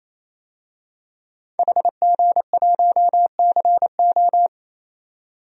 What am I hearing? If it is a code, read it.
Morse code, heard as a keyed tone, about 28 wpm, 700 Hz. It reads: HG1CO
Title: DXCC-100 Call Signs - 5 Characters - 28wpm